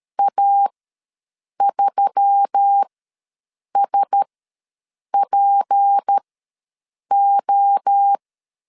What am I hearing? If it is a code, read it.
Morse code, heard as a keyed tone, about 13 wpm, spaced as for 7 wpm, 800 Hz. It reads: A3SPO
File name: LEZIONI / 6 PKX0 / 6 PKX0 ws 12wpm 3448